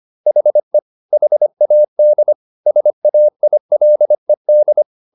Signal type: Morse code